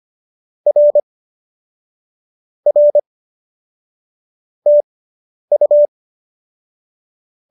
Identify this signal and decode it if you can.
Morse code, heard as a keyed tone, about 25 wpm, spaced as for 10 wpm, 570 Hz. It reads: R R TU